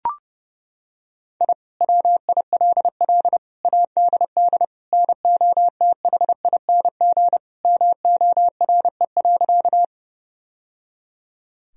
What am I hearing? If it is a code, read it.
Morse code, heard as a keyed tone, about 32 wpm, 700 Hz. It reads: I WILL ADD NOTHING MORE.
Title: Sentences from Top 1000 Words - Review Part 1 - 30wpm